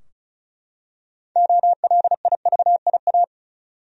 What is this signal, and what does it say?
Morse code, heard as a keyed tone, about 36 wpm, 700 Hz. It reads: OLIVIA
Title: USA Common Names - 35wpm